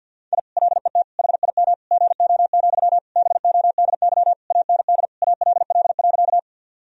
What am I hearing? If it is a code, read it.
Morse code, heard as a keyed tone, about 46 wpm, 700 Hz. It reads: I LET HIM GO, BODY AND ALL.
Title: Sentences from Top 1000 Words - Review Part 2 - 50wpm